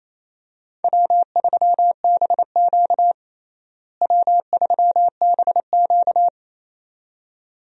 Morse code, heard as a keyed tone, about 28 wpm, 700 Hz. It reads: W3BQ W3BQ
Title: USA Call Signs - 4 Characters - Repeated 2x - 28wpm